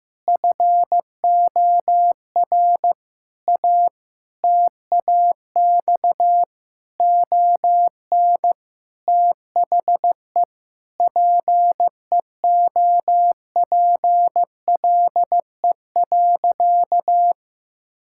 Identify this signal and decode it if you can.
Morse code, heard as a keyed tone, about 15 wpm, 700 Hz. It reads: FOR A TAX ON THE PEOPLE.
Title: Sentences from Top 1000 Words - Review Part 5 - 15wpm